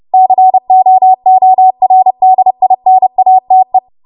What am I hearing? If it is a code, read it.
Morse code, heard as a keyed tone, about 30 wpm, 750 Hz. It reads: COORDINATE
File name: Morse / 30WPM